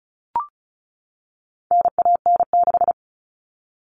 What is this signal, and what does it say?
Morse code, heard as a keyed tone, about 34 wpm, 700 Hz. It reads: NAN6
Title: All Cut Numbers - 4-Digit 35wpm